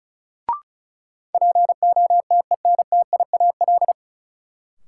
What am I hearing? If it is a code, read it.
Morse code, heard as a keyed tone, about 34 wpm, 700 Hz. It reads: POTENTIAL
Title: Sentences from Top 1000 Words - Review Part 4 - New Word Drill - 35wpm